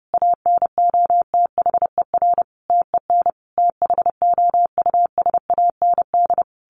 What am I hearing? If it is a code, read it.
Morse code, heard as a keyed tone, about 30 wpm, 700 Hz. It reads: ANOTHER TEN THOUSAND
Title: A Sound Of Thunder 30wpm